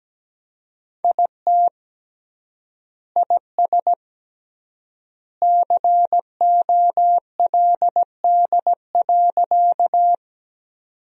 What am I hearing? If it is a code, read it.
Morse code, heard as a keyed tone, about 17 wpm, 700 Hz. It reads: IT IS COLD.
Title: 3x Word-Spacing - Sentences from Top 400 Words - 17wpm